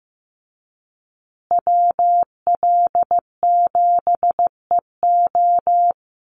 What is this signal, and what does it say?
Morse code, heard as a keyed tone, about 15 wpm, 700 Hz. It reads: WL7EO
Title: USA Call Signs Repeated 3x 15wpm